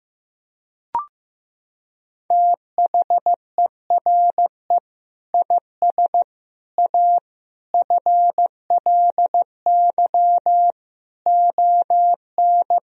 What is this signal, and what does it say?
Morse code, heard as a keyed tone, about 15 wpm, 700 Hz. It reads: THERE IS A FLY ON
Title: Sentences from Top 1000 Words - Review Part 2 - 15wpm